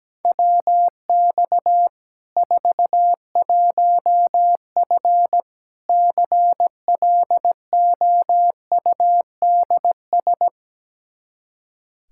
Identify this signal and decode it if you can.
Morse code, heard as a keyed tone, about 17 wpm, 700 Hz. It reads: WX 41F CLOUDS
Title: Practice QSO - Exchange 2 - Info - 17wpm